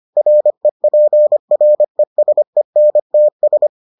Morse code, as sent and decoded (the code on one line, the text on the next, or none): .-. .--. .-. ... -. - ...
REPRESENTS